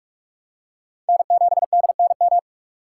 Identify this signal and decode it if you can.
Morse code, heard as a keyed tone, about 46 wpm, 700 Hz. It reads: N7DNM
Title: USA Call Signs - 6x Character Spacing - 45wpm